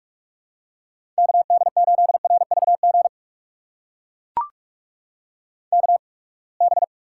Morse code, heard as a keyed tone, about 43 wpm, spaced as for 30 wpm, 700 Hz. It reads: KD8RVG KB
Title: USA Call Signs - 6 Characters - 8x Character Spacing - 45wpm